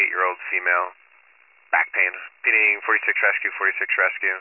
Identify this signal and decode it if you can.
Emergency Alert System (SAME) tones, heard as harsh buzzing bursts, with speech. It is heard from 0.4 to 3.5 s.